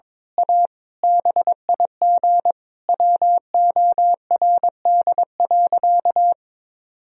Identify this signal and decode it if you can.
Morse code, heard as a keyed tone, about 22 wpm, 700 Hz. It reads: A BIG WORD.
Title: Sentences from Top 1000 Words - Review Part 1 - 22wpm